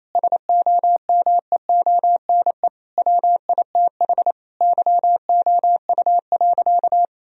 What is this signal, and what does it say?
Morse code, heard as a keyed tone, about 28 wpm, 700 Hz. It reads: SOMEONE WITH YOU.
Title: Sentences from Top 1000 Words - Review Part 4 - 28wpm